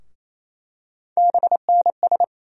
Morse code, heard as a keyed tone, about 27 wpm, 700 Hz. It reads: BNS
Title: USA Call Sign Postfixes - 3 Letters 28wpm